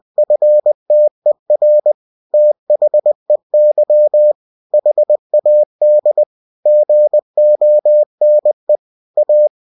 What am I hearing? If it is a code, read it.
Morse code, heard as a keyed tone, about 20 wpm, 600 Hz. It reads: FTER THEY HAD GONE A
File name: cw-books / winnie-the-pooh / winnie-20wpm-page-0083